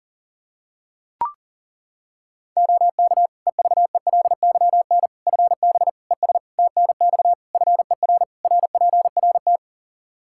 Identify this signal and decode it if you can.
Morse code, heard as a keyed tone, about 40 wpm, 700 Hz. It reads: OK EVELYN FB ES TNX FER RPRT
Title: Practice QSO - Exchange 2 - Info - 40wpm